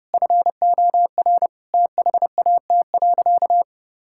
Morse code, heard as a keyed tone, about 32 wpm, 700 Hz. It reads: FOR THAT.
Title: Sentences from Top 1000 Words - Review Part 5 - 30wpm